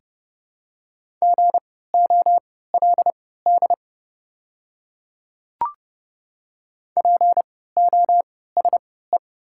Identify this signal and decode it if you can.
Morse code, heard as a keyed tone, about 30 wpm, spaced as for 17 wpm, 700 Hz. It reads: GOLD POSE
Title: 3x Character Spacing - Top 500 Words - 30wpm